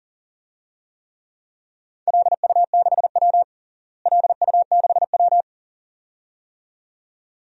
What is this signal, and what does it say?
Morse code, heard as a keyed tone, about 40 wpm, 700 Hz. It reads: LU6W LU6W
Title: DXCC-100 Call Signs - 4 Characters - Repeated 2x - 40wpm